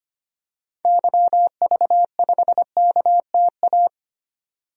Morse code, heard as a keyed tone, about 25 wpm, 700 Hz. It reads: YV5KTA
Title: DXCC-100 Call Signs - 6 Characters - 25wpm